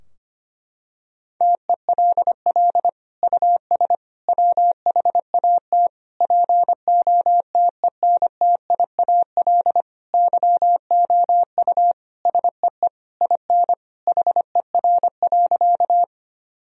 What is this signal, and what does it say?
Morse code, heard as a keyed tone, about 25 wpm, 700 Hz. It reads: TELL US WHAT POTENTIAL YOU SEE IN HER.
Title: Sentences from Top 1000 Words - Review Part 4 - 25wpm